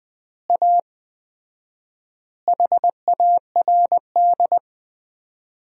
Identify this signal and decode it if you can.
Morse code, heard as a keyed tone, about 20 wpm, 700 Hz. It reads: A HARD
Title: 4x Word-Spacing - Sentences from Top 300 Words - 20wpm